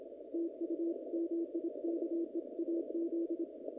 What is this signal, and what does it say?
Morse code, heard as a keyed tone, about 26 wpm, 340 Hz. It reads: TU MIKE AZ
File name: contests / naqp-cw-k5tr-op-k5pi- -2000